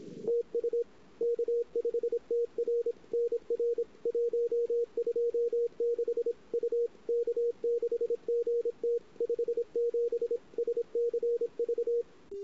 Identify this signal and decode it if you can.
Morse code, heard as a keyed tone, about 29 wpm, 470 Hz. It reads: TU K5TR NR 126 U K6GT 57 SCV
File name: CW / 2008-ss-cw-k5tr-op-n5rz -23:00